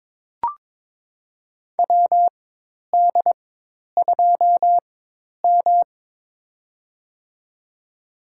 Morse code, heard as a keyed tone, about 22 wpm, spaced as for 10 wpm, 700 Hz. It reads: WD2M
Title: USA Call Signs - 4 Characters - 4x Character Spacing - 22wpm